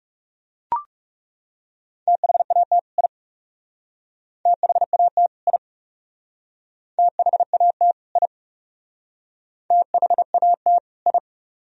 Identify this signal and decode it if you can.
Morse code, heard as a keyed tone, about 37 wpm, 700 Hz. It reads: THAT I THAT I THAT I THAT I
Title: Speed-Racing - Sets of 2 Words 30wpm